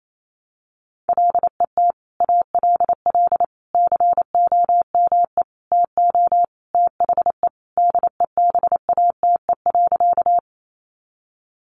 Morse code, heard as a keyed tone, about 28 wpm, 700 Hz. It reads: LET ALL COME TO THE DEBATE.
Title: Sentences from Top 1100 Words 28wpm